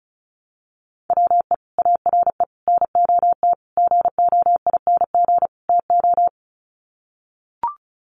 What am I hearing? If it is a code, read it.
Morse code, heard as a keyed tone, about 34 wpm, 700 Hz. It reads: WE ARE NOT GOING TO
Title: Sets of 5 Words 35wpm